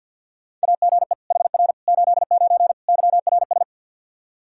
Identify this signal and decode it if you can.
Morse code, heard as a keyed tone, about 53 wpm, 700 Hz. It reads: AGE HR 89 YRS
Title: Practice QSO - Exchange 3 - More Info - 50wpm